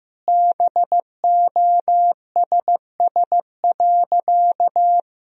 Morse code, heard as a keyed tone, about 15 wpm, 700 Hz. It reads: BOSS.
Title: Sentences from Top 1000 Words - Review Part 4 - 15wpm